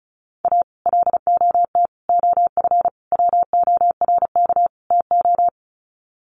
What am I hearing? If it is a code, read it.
Morse code, heard as a keyed tone, about 34 wpm, 700 Hz. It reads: A LOT OF WORK TO
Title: Sets of 5 Words Encore2 35wpm